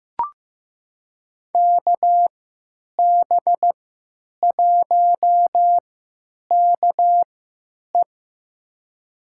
Morse code, heard as a keyed tone, about 16 wpm, spaced as for 9 wpm, 700 Hz. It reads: KB1KE
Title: USA Call Signs - 5 Characters - 3x Character Spacing - 15wpm